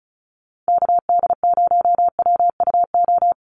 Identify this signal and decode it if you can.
Morse code, heard as a keyed tone, about 34 wpm, 700 Hz. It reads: KD0WUO
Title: USA Call Signs Repeated 3x 35wpm